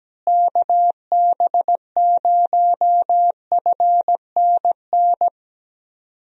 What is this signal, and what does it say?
Morse code, heard as a keyed tone, about 17 wpm, 700 Hz. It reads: KB0FNN
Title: USA Call Signs - 6 Characters - Repeated 2x - 17wpm